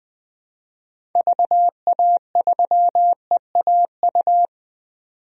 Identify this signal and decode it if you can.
Morse code, heard as a keyed tone, about 20 wpm, 700 Hz. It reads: VA3EAU
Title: DXCC-100 Call Signs - 6 Characters - 20wpm